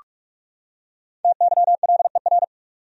Morse code, heard as a keyed tone, about 46 wpm, 700 Hz. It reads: TYLER
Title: USA Common Names - Max 5 Letters - 45wpm